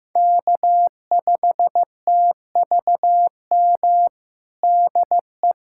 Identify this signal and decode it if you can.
Morse code, heard as a keyed tone, about 16 wpm, 700 Hz. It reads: K5TVM DE
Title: USA Callsign DE Callsign 15wpm